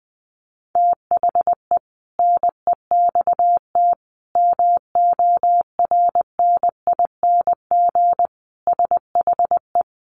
Morse code, heard as a keyed tone, about 19 wpm, 700 Hz. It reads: THE NEXT MORNING SHE
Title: Sentences from Top 1200 Words 20wpm